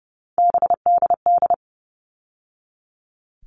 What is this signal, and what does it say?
Morse code, heard as a keyed tone, about 30 wpm, 700 Hz. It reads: BDD
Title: All Cut Numbers - 3-Digit 30wpm